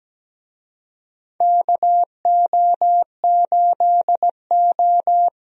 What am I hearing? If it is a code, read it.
Morse code, heard as a keyed tone, about 17 wpm, 700 Hz. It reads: KO8O
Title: USA Call Signs - 4 Characters - Repeated 2x - 17wpm